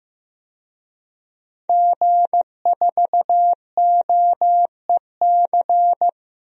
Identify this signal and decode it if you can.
Morse code, heard as a keyed tone, about 15 wpm, 700 Hz. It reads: G4OEC
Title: DXCC-100 Call Signs - 5 Characters - Repeated 2x - 15wpm